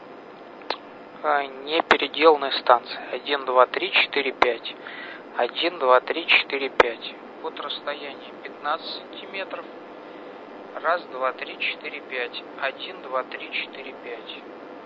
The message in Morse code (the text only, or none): none